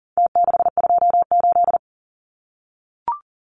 Morse code, heard as a keyed tone, about 40 wpm, 700 Hz. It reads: T628
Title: Numbers - Contest Style - 4-Digit 40wpm